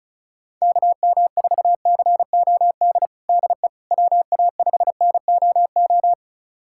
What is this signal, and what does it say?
Morse code, heard as a keyed tone, about 36 wpm, 700 Hz. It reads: KM4COD DE WA5NOO
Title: USA Callsign DE Callsign - 6 Characters - 35wpm